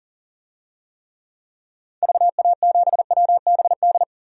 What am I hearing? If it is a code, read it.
Morse code, heard as a keyed tone, about 38 wpm, 700 Hz. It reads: VA7WBD
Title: DXCC-100 Call Signs - 6 Characters - 40wpm